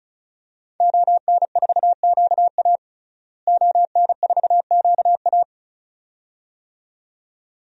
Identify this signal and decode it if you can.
Morse code, heard as a keyed tone, about 36 wpm, 700 Hz. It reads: ON4QA ON4QA
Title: DXCC-100 Call Signs - 5 Characters - Repeated 2x - 35wpm